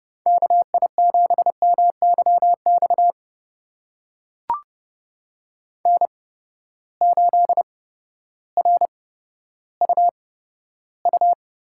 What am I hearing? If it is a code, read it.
Morse code, heard as a keyed tone, about 30 wpm, spaced as for 14 wpm, 700 Hz. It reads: KI7MYX N8RUU
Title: USA Call Signs - 8x Character Spacing - 30wpm